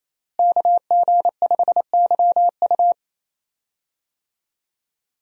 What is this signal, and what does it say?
Morse code, heard as a keyed tone, about 28 wpm, 700 Hz. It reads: KG5YU
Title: USA Call Signs - 5 Characters - Repeated 2x - 28wpm